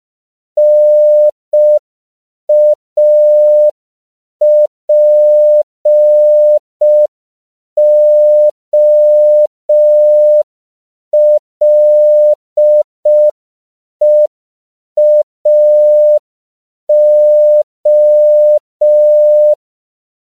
Morse code, heard as a keyed tone, about 5 wpm, 600 Hz. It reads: NAPOLEAO
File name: CW / 5PPM